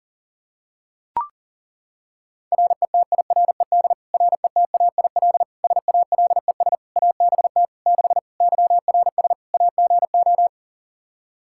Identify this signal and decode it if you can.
Morse code, heard as a keyed tone, about 37 wpm, 700 Hz. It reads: RETIRED RETAIL SALES ABT 6 YRS AGO
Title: Practice QSO - Exchange 3 - More Info - 40wpm